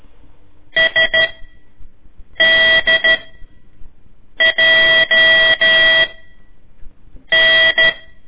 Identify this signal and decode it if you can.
Morse code, heard as a keyed tone, about 13 wpm, spaced as for 6 wpm, 1950 Hz. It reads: SDJN
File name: ik1weg2 / CORSI CW / lez4-1